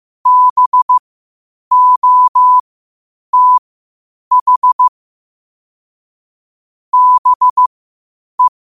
Morse code, heard as a keyed tone, about 15 wpm, spaced as for 8 wpm, 1000 Hz. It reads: BOTH BE
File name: qotdfiles / 8wpm